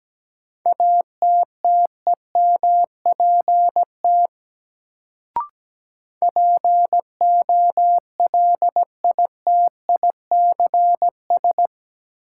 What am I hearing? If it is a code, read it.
Morse code, heard as a keyed tone, about 17 wpm, 700 Hz. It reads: ATTEMPT POLITICS
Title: Sentences from Top 1000 Words - New Word Drill - 17wpm